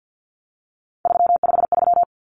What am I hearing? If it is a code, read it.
Morse code, heard as a keyed tone, about 50 wpm, 700 Hz. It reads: 353
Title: Common Cut Numbers - 3-Digit 50wpm